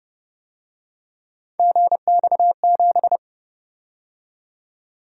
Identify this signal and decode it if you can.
Morse code, heard as a keyed tone, about 29 wpm, 700 Hz. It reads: GX7